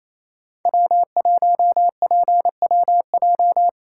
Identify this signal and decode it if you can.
Morse code, heard as a keyed tone, about 28 wpm, 700 Hz. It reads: W1PWJ
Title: USA Call Signs - 5 Characters - 4x Character Spacing - 28wpm